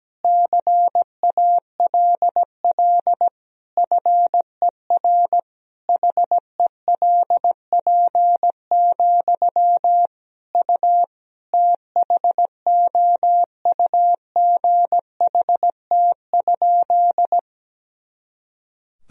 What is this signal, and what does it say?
Morse code, heard as a keyed tone, about 17 wpm, 700 Hz. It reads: CALL FER HELP, U THOUGHT?
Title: Sentences from Top 300 Words and Abbreviations 17wpm